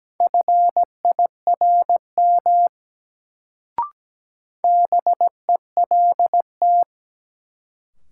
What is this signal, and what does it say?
Morse code, heard as a keyed tone, about 17 wpm, 700 Hz. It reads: FIRM BELT
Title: Sentences from Top 1400 Words - New Word Drill - 17wpm